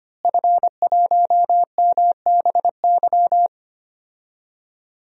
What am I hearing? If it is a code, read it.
Morse code, heard as a keyed tone, about 25 wpm, 700 Hz. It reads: F1MBY
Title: DXCC-100 Call Signs - 5 Characters - Repeated 3x - 25wpm